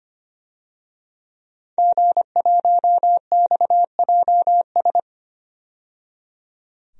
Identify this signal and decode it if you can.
Morse code, heard as a keyed tone, about 25 wpm, 700 Hz. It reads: G1XJS
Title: DXCC-100 Call Signs - 5 Characters - 25wpm